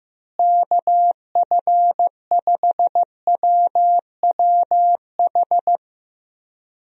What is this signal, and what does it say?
Morse code, heard as a keyed tone, about 15 wpm, 700 Hz. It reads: KF5WWH
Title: USA Call Signs - 6 Characters - Repeated 2x - 15wpm